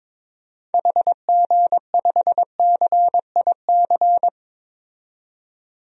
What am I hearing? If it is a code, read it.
Morse code, heard as a keyed tone, about 22 wpm, 700 Hz. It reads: HG5CIC
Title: DXCC-100 Call Signs - 6 Characters - 22wpm